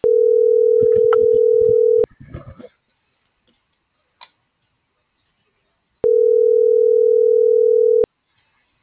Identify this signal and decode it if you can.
Ringback tone (North America), 440 + 480 Hz.